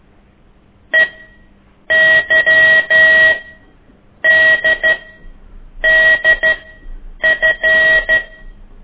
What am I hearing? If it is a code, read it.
Morse code, heard as a keyed tone, about 12 wpm, spaced as for 7 wpm, 1900 Hz. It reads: EYDDF
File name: ik1weg2 / CORSI CW